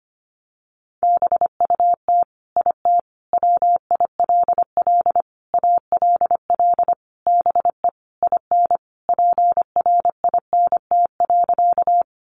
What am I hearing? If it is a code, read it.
Morse code, heard as a keyed tone, about 25 wpm, 700 Hz. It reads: BUT IT WILL ALL BE IN PRINT.
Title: Sentences from Top 1400 Words 25wpm